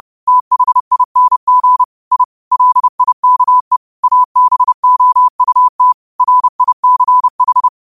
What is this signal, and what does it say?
Morse code, heard as a keyed tone, about 30 wpm, 1000 Hz. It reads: THING I LIKE ABOUT RICH
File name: qotdfiles / 30wpm